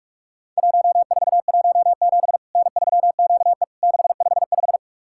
Morse code, heard as a keyed tone, about 44 wpm, 700 Hz. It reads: 1417 N3QE 655
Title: Worked All Europe DX Contest - QTC Exchange - 45wpm